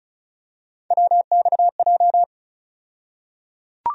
Morse code, heard as a keyed tone, about 36 wpm, 700 Hz. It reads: WXJ
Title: USA Call Sign Postfixes - 3 Letters 35wpm